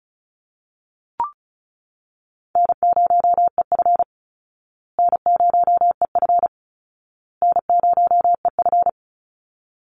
Morse code, heard as a keyed tone, about 34 wpm, 700 Hz. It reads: N0EF N0EF N0EF
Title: Easy Call Signs Repeated 3x - 35wpm